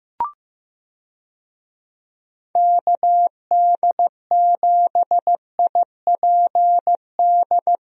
Morse code, heard as a keyed tone, about 15 wpm, 700 Hz. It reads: KD7IPD
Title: USA Call Signs - 6 Characters - Repeated 2x - 15wpm